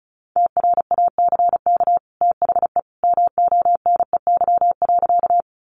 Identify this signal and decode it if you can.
Morse code, heard as a keyed tone, about 34 wpm, 700 Hz. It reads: TRACK THE MONEY.
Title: Sentences from Top 1100 Words 35wpm